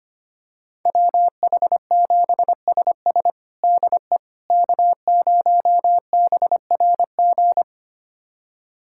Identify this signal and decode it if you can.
Morse code, heard as a keyed tone, about 25 wpm, 700 Hz. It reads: WH7SS DE K0BRG
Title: USA Callsign DE Callsign - 5 Characters - 25wpm